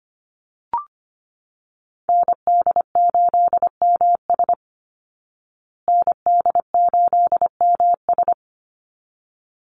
Morse code, heard as a keyed tone, about 25 wpm, 700 Hz. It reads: ND8MS ND8MS
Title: Call Signs Repeated 3x - 25wpm